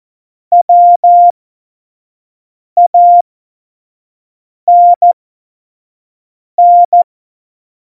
Morse code, heard as a keyed tone, about 14 wpm, spaced as for 5 wpm, 700 Hz. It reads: WANN
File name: Morsekurs / Witze / Witz 13. Februar 2018, 25 F70